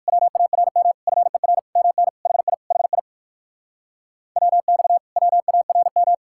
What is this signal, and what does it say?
Morse code, heard as a keyed tone, about 59 wpm, 700 Hz. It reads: WARM FER MN HI HI WX WARM